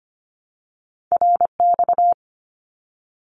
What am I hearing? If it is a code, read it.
Morse code, heard as a keyed tone, about 25 wpm, 700 Hz. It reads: RX
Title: Abbreviations 25wpm